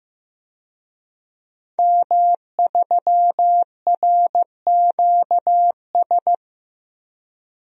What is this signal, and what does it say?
Morse code, heard as a keyed tone, about 15 wpm, 700 Hz. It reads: M3RQS
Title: DXCC-100 Call Signs - 5 Characters - 15wpm